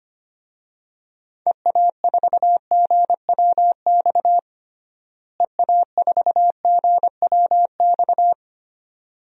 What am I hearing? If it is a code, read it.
Morse code, heard as a keyed tone, about 25 wpm, 700 Hz. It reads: EA4GWX EA4GWX
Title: DXCC-100 Call Signs - 6 Characters - Repeated 2x - 25wpm